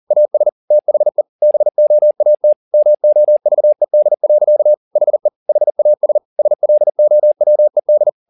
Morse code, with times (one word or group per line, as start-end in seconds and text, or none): 0.1-0.5 AS
0.7-1.2 THE
1.4-2.5 BOAT
2.7-4.7 MOVED.
5.0-5.3 HE
5.5-6.2 HAS
6.4-8.1 SLOWED